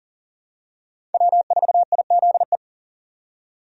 Morse code, heard as a keyed tone, about 40 wpm, 700 Hz. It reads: W4IZE
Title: USA Call Signs - 5 Characters - 40wpm